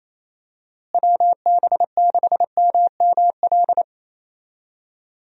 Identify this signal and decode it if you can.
Morse code, heard as a keyed tone, about 27 wpm, 700 Hz. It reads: WB6MML
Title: USA Call Signs - 6 Characters - Repeated 3x - 28wpm